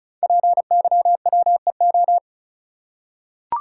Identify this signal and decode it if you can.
Morse code, heard as a keyed tone, about 36 wpm, 690 Hz. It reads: PYWEO
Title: Code Groups - 5 Letters - 35wpm